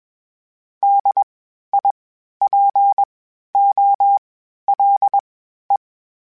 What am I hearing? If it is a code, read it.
Morse code, heard as a keyed tone, about 21 wpm, spaced as for 12 wpm, 800 Hz. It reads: DIPOLE
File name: (spaced) / FromSMs spaced mp3s / Antennas sp@25